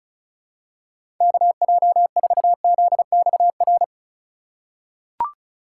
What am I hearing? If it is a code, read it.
Morse code, heard as a keyed tone, about 34 wpm, 700 Hz. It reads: KJ4ZXR
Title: USA Call Signs - 6 Characters - 2x Character Spacing - 35wpm